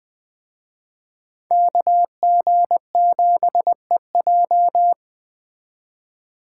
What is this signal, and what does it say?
Morse code, heard as a keyed tone, about 20 wpm, 700 Hz. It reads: KG7EJ